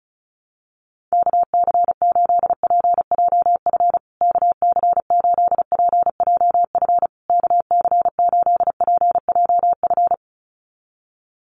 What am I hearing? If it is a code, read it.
Morse code, heard as a keyed tone, about 34 wpm, 700 Hz. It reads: KC8PJF KC8PJF KC8PJF
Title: USA Call Signs Repeated 3x 35wpm